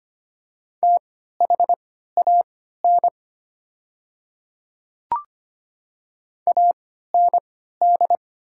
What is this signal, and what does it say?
Morse code, heard as a keyed tone, about 25 wpm, spaced as for 14 wpm, 700 Hz. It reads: THAN AND
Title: 3x Character Spacing - Top 100 Words - 25wpm